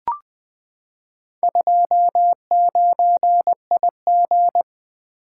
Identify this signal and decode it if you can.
Morse code, heard as a keyed tone, about 20 wpm, 700 Hz. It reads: 29IG
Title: Four Letter-Number Combinations - 20wpm